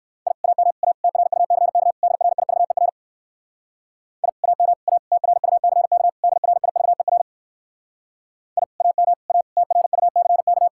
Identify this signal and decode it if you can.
Morse code, heard as a keyed tone, about 58 wpm, 700 Hz. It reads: I AM A TRUCK DRIVER I AM A TRUCK DRIVER I AM A TRUCK